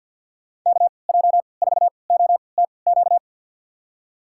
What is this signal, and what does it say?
Morse code, heard as a keyed tone, about 49 wpm, spaced as for 28 wpm, 690 Hz. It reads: KJ4OTQ
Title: USA Call Signs - 6 Characters - 3x Character Spacing - 50wpm